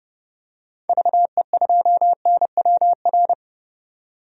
Morse code, heard as a keyed tone, about 30 wpm, 700 Hz. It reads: VE2NWR
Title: DXCC-100 Call Signs - 6 Characters - 30wpm